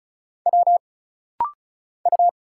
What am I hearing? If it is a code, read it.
Morse code, heard as a keyed tone, about 36 wpm, 700 Hz. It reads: W U
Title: Single Letters - Rapid-Fire 35wpm